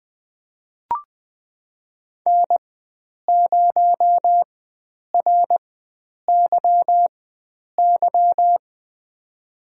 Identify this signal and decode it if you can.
Morse code, heard as a keyed tone, about 20 wpm, spaced as for 9 wpm, 700 Hz. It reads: N0RYY
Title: USA Call Signs - 5 Characters - 4x Character Spacing - 20wpm